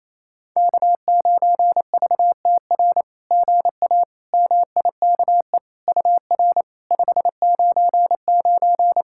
Morse code, heard as a keyed tone, about 28 wpm, 700 Hz. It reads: K9VTR GA MIKE UR 599
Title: SOTA QSO Practice 28wpm